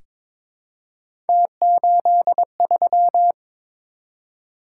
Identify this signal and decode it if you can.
Morse code, heard as a keyed tone, about 21 wpm, 700 Hz. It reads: T83
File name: practice / 22wpm / Three Letter-Number Combinations - 22wpm